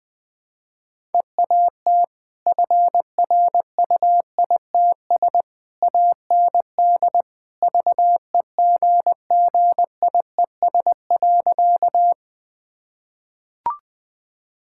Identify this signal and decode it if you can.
Morse code, heard as a keyed tone, about 21 wpm, 700 Hz. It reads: EAT FRUITS AND VEGGIES.
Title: Easy Sentences with Hints - 20wpm